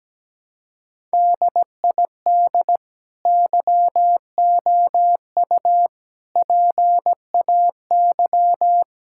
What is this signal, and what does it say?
Morse code, heard as a keyed tone, about 17 wpm, 700 Hz. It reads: DID YOU PAY
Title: Sentences from Top 1000 Words - Review Part 5 - 17wpm